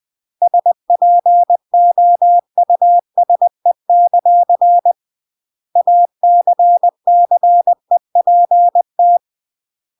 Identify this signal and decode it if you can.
Morse code, heard as a keyed tone, about 20 wpm, 700 Hz. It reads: SPOUSE; ACCEPT
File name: CW / jodorowsky / 82 commandments 20-20